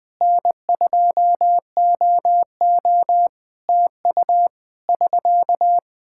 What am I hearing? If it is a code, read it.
Morse code, heard as a keyed tone, about 20 wpm, 700 Hz. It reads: N2OO TU <SK>
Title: K1USN SST QSO Practice 20wpm